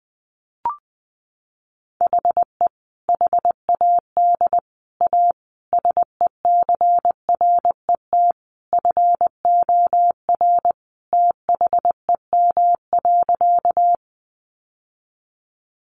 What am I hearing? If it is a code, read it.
Morse code, heard as a keyed tone, about 20 wpm, 700 Hz. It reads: HE HAD A SECRET FOR THEM.
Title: Sentences from Top 1200 Words 20wpm